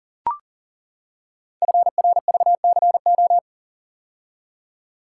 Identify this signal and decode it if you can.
Morse code, heard as a keyed tone, about 40 wpm, 700 Hz. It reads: FRVCO
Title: Code Groups - 5 Letters - 40wpm